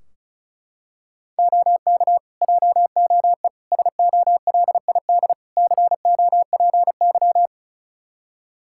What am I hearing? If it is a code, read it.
Morse code, heard as a keyed tone, about 34 wpm, 700 Hz. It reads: OK JOE SOLID COPY
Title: Practice QSO - Exchange 3 - More Info - 35wpm